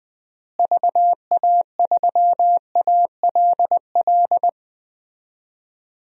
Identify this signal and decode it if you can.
Morse code, heard as a keyed tone, about 20 wpm, 700 Hz. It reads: VA3ALL
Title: DXCC-100 Call Signs - 6 Characters - Repeated 2x - 20wpm